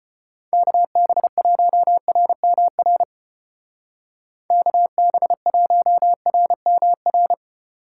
Morse code, heard as a keyed tone, about 29 wpm, 700 Hz. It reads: KB1RMR KB1RMR
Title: Speed Racing - USA 2x3 Call Signs 25wpm